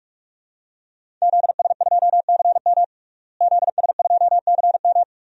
Morse code, heard as a keyed tone, about 47 wpm, 700 Hz. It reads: ZS2CM ZS2CM
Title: DXCC-100 Call Signs - 5 Characters - Repeated 2x - 45wpm